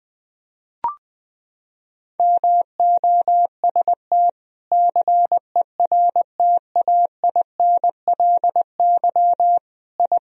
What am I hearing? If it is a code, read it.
Morse code, heard as a keyed tone, about 21 wpm, 700 Hz. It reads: MOST CERTAINLY I
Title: Sentences from Top 1000 Words - Review Part 1 - 20wpm